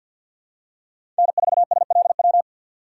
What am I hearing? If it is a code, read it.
Morse code, heard as a keyed tone, about 53 wpm, 700 Hz. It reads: N4SLW
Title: USA Call Signs - 5 Characters - 6x Character Spacing - 50wpm